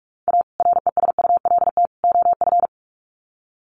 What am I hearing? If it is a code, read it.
Morse code, heard as a keyed tone, about 44 wpm, 700 Hz. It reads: A RESULT OF
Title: Sets of 3 Words 45wpm